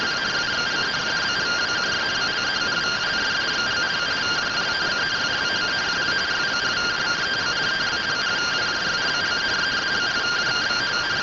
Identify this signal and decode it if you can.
Two-tone teleprinter signal, about 170 Hz shift.